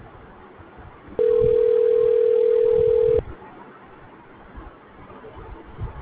Ringback tone (North America), 440 + 480 Hz.
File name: recordings / RE0d6a4fed0b3baf8fed7f5032fc9fdbf9